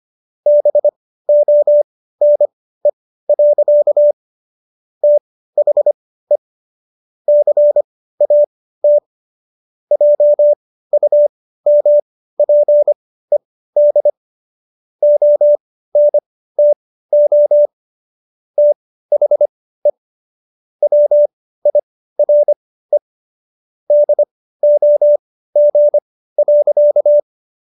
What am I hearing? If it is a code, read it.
Morse code, heard as a keyed tone, about 25 wpm, spaced as for 15 wpm, 590 Hz. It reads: BONE. THE CAT JUMPED ONTO THE WIRE DOG.